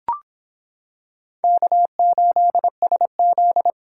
Morse code, heard as a keyed tone, about 26 wpm, 700 Hz. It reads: K8SZ